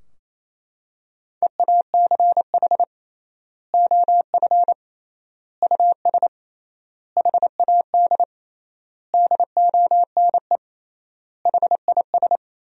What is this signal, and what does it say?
Morse code, heard as a keyed tone, about 28 wpm, 700 Hz. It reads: EACH OF US HAD DONE HIS
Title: 3x Word-Spacing - Sentences from Top 500 Words - 28wpm